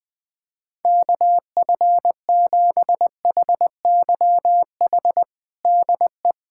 Morse code, heard as a keyed tone, about 20 wpm, 700 Hz. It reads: KF7HYH DE
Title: USA Callsign DE Callsign - 6 Characters - 20wpm